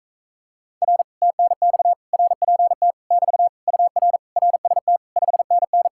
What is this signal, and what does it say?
Morse code, heard as a keyed tone, about 38 wpm, 700 Hz. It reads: R TNX RPT <BT> UR RST 5NN